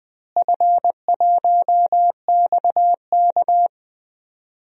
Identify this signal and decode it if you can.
Morse code, heard as a keyed tone, about 20 wpm, 700 Hz. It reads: F1XK
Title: DXCC-100 Call Signs - 4 Characters - Repeated 2x - 20wpm